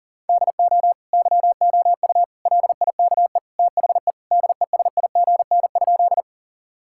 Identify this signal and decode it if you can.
Morse code, heard as a keyed tone, about 37 wpm, 700 Hz. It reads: DO YOU LIKE THE DESIGN?
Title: Sentences from Top 1000 Words - Review Part 5 - 40wpm